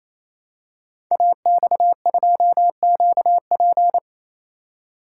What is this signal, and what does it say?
Morse code, heard as a keyed tone, about 29 wpm, 700 Hz. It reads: AX2QP
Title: Code Groups - 5 Letter:Numbers - 28wpm